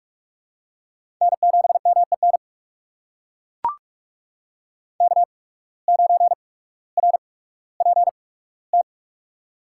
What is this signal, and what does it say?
Morse code, heard as a keyed tone, about 42 wpm, spaced as for 19 wpm, 700 Hz. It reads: N7MEN K9RPT
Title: USA Call Signs - 5 Characters - 8x Character Spacing - 45wpm